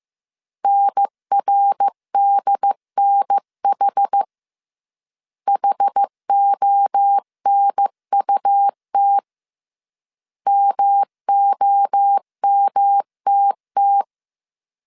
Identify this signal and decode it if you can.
Morse code, heard as a keyed tone, about 15 wpm, 800 Hz. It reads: NRDNH HONUT MOMTT
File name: CW / DET / 1 3 1500C S A 14wpm 2447